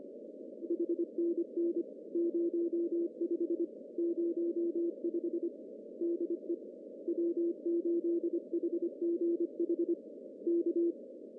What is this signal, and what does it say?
Morse code, heard as a keyed tone, about 25 wpm, 340 Hz. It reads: HNN 05 05 DE W8HGH K
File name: CW / cqww-cw-k5tr